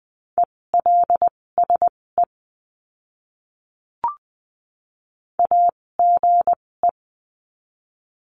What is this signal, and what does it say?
Morse code, heard as a keyed tone, about 20 wpm, spaced as for 16 wpm, 700 Hz. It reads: ELSE AGE
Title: 256 Lesson 42a - Words 12-20wpm